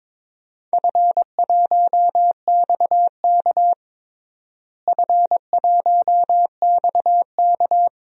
Morse code, heard as a keyed tone, about 22 wpm, 700 Hz. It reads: F1XK F1XK
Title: DXCC-100 Call Signs - 4 Characters - Repeated 2x - 22wpm